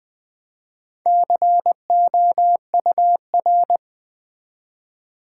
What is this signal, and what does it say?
Morse code, heard as a keyed tone, about 20 wpm, 700 Hz. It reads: COUR
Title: Build A Word - 1 Letter At A Time - 5-Letter Words - 20wpm